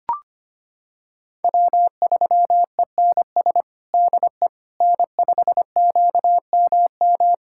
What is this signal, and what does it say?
Morse code, heard as a keyed tone, about 25 wpm, 700 Hz. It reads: W3ENS DE N5QMM
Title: USA Callsign DE Callsign - 5 Characters - 25wpm